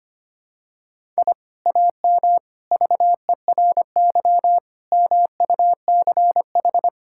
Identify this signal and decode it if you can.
Morse code, heard as a keyed tone, about 25 wpm, 700 Hz. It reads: I AM VERY MUCH
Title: Sentences from Top 1000 Words - Review Part 6 - 25wpm